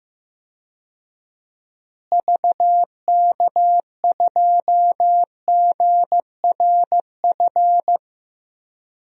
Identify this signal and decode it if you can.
Morse code, heard as a keyed tone, about 15 wpm, 700 Hz. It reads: VK2GRF
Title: DXCC-100 Call Signs - 6 Characters - Repeated 3x - 15wpm